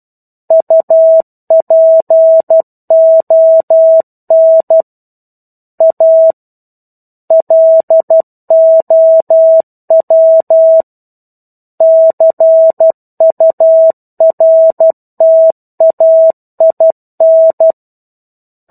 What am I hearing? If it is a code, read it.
Morse code, heard as a keyed tone, about 12 wpm, 650 Hz. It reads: UPON A LOW CURTAIN
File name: Chapter-01 12 WPM